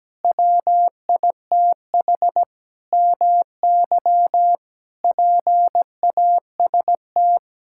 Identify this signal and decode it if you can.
Morse code, heard as a keyed tone, about 17 wpm, 700 Hz. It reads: WITH MY PAST
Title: Sentences from Top 1000 Words - Review Part 6 - 17wpm